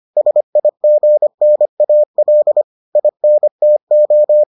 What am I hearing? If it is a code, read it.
Morse code, heard as a keyed tone, about 25 wpm, 600 Hz. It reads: SIGNAL INTO